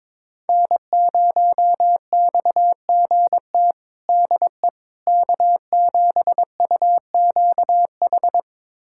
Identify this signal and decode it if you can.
Morse code, heard as a keyed tone, about 22 wpm, 700 Hz. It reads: N0XGT DE K7UQH
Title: USA Callsign DE Callsign - 5 Characters - 22wpm